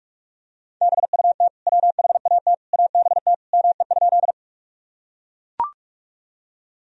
Morse code, heard as a keyed tone, about 43 wpm, 700 Hz. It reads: BUT WHAT ABT ME?
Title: Sentences from Top 200 Words and Abbreviations 45wpm